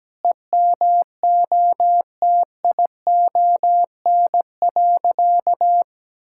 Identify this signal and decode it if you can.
Morse code, heard as a keyed tone, about 17 wpm, 700 Hz. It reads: EMOTION.
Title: Sentences from Top 1600 Words 17wpm